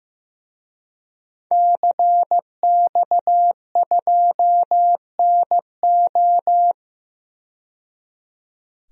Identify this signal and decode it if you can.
Morse code, heard as a keyed tone, about 15 wpm, 700 Hz. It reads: CX2NO
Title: DXCC-100 Call Signs - 5 Characters - 15wpm